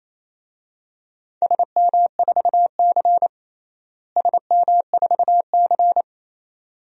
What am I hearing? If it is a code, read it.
Morse code, heard as a keyed tone, about 28 wpm, 700 Hz. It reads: SM4C SM4C
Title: DXCC-100 Call Signs - 4 Characters - Repeated 2x - 28wpm